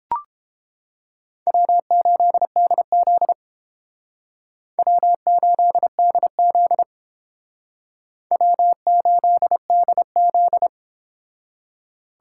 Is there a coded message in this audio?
Yes: Morse code from 1.5 to 10.7 s, heard as a keyed tone, about 29 wpm, 700 Hz. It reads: W8DZ W8DZ W8DZ